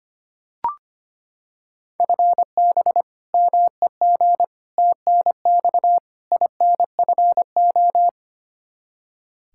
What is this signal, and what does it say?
Morse code, heard as a keyed tone, about 25 wpm, 700 Hz. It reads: FB MEG TNX INFO
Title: Practice QSO - Exchange 3 - More Info - 25wpm